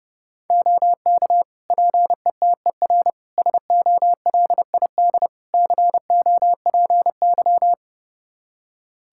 Morse code, heard as a keyed tone, about 29 wpm, 700 Hz. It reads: OK PETER SOLID COPY
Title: K1USN SST QSO Practice 30wpm